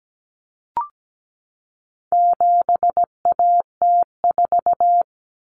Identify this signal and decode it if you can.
Morse code, heard as a keyed tone, about 17 wpm, 700 Hz. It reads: 7AT4
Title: Common Cut Numbers - 4-Digit 17wpm